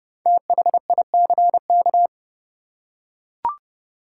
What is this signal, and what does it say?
Morse code, heard as a keyed tone, about 32 wpm, 700 Hz. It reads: THICK
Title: Five Letter Words - 30wpm